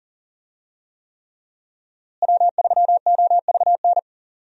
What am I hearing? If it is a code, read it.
Morse code, heard as a keyed tone, about 40 wpm, 700 Hz. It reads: W3OVN
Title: USA Call Signs - 5 Characters - Repeated 3x - 40wpm